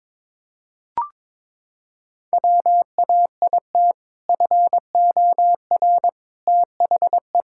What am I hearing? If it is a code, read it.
Morse code, heard as a keyed tone, about 22 wpm, 700 Hz. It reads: WAIT FOR THE
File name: practice / 22wpm / Sentences with 2-Word Phrases - 22wpm